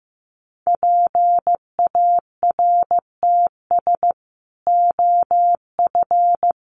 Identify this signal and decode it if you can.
Morse code, heard as a keyed tone, about 15 wpm, 700 Hz. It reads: PARTS OF
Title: Sets of 3 Words 15wpm